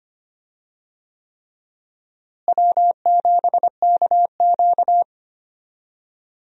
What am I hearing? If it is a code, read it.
Morse code, heard as a keyed tone, about 25 wpm, 700 Hz. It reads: W7KQ
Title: USA Call Signs - 4 Characters - Repeated 3x - 25wpm